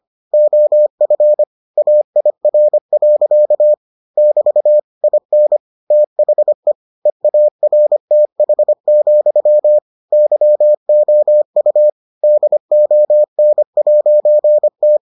Morse code, heard as a keyed tone, about 25 wpm, 600 Hz. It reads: OF AIR. <BT> IN THE EARTH, YOU DON'T